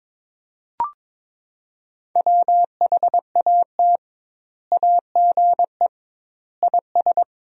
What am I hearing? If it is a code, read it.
Morse code, heard as a keyed tone, about 22 wpm, 700 Hz. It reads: WHAT AGE IS
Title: 2x Word-Spacing - Sentences from Top 500 Words - 22wpm